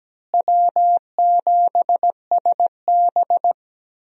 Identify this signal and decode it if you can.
Morse code, heard as a keyed tone, about 17 wpm, 700 Hz. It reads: W7SB
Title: USA Call Signs - 4 Characters - Repeated 2x - 17wpm